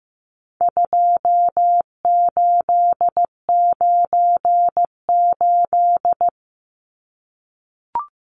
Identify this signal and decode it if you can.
Morse code, heard as a keyed tone, about 15 wpm, 700 Hz. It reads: 2898
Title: Numbers - 4-Digit 15wpm